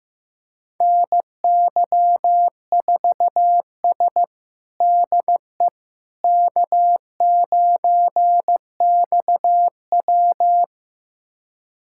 Morse code, heard as a keyed tone, about 15 wpm, 700 Hz. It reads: NY4S DE K9XW
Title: USA Callsign DE Callsign - 4 Characters - 15wpm